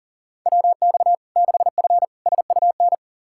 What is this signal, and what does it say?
Morse code, heard as a keyed tone, about 40 wpm, 700 Hz. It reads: WX 6F SUN
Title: Practice QSO - Exchange 2 - Info - 40wpm